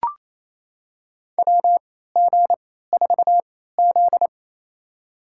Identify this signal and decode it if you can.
Morse code, heard as a keyed tone, about 28 wpm, spaced as for 16 wpm, 700 Hz. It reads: WG4Z
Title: USA Call Signs - 4 Characters - 3x Character Spacing - 28wpm